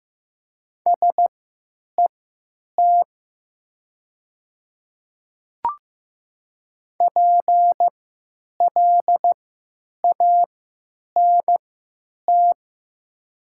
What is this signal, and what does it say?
Morse code, heard as a keyed tone, about 15 wpm, spaced as for 9 wpm, 700 Hz. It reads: SET PLANT
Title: 3x Character Spacing - Top 500 Words - 15wpm